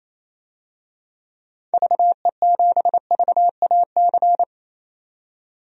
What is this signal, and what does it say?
Morse code, heard as a keyed tone, about 28 wpm, 700 Hz. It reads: VE7VAC
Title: DXCC-100 Call Signs - 6 Characters - 28wpm